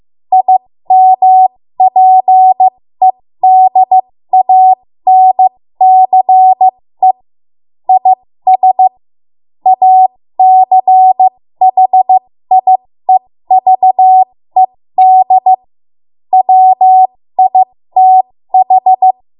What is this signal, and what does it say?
Morse code, heard as a keyed tone, about 15 wpm, spaced as for 13 wpm, 750 Hz. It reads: IMPEDANCE IS ACHIEVED WITH
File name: Morse / 13WPM